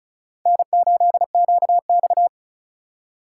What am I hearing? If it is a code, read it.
Morse code, heard as a keyed tone, about 36 wpm, 700 Hz. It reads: N8QX
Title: USA Call Signs - 4 Characters - 2x Character Spacing - 35wpm